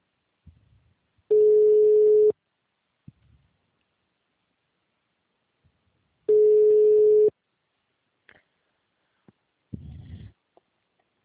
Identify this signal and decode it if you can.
Ringback tone (Europe, 425 Hz).